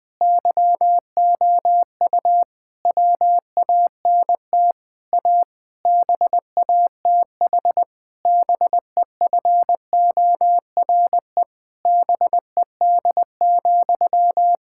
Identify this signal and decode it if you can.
Morse code, heard as a keyed tone, about 20 wpm, 700 Hz. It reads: YOU WANT A BATH BEFORE BED,
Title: Sentences from Top 1600 Words 20wpm